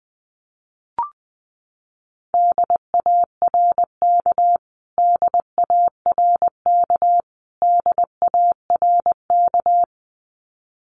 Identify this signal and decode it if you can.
Morse code, heard as a keyed tone, about 19 wpm, 700 Hz. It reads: DARK DARK DARK
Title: Top 500 Words Repeated 20wpm